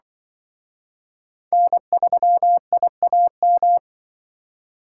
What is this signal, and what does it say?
Morse code, heard as a keyed tone, about 23 wpm, 700 Hz. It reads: N3IAM
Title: Speed Racing - USA 1x3 Call Signs 20wpm